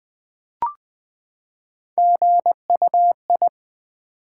Morse code, heard as a keyed tone, about 20 wpm, 700 Hz. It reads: GUI
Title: Build A Word - 3 Letters At A Time - 8-Letter Words - 20wpm